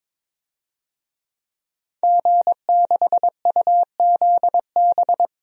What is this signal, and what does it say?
Morse code, heard as a keyed tone, about 22 wpm, 700 Hz. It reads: G6UZB
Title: DXCC-100 Call Signs - 5 Characters - 22wpm